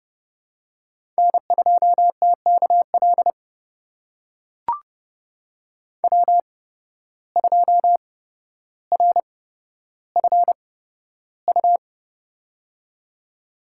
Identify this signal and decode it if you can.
Morse code, heard as a keyed tone, about 29 wpm, spaced as for 13 wpm, 700 Hz. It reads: N2TKL W2RFU
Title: USA Call Signs - 5 Characters - 8x Character Spacing - 30wpm